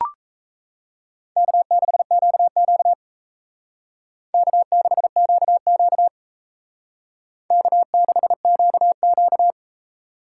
Morse code, heard as a keyed tone, about 34 wpm, 700 Hz. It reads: K6QQ K6QQ K6QQ
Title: Speed Racing - USA 1x2 Call Signs 28wpm